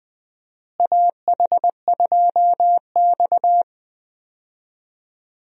Morse code, heard as a keyed tone, about 21 wpm, 700 Hz. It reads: AH2X
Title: USA Call Signs - 4 Characters - Repeated 2x - 20wpm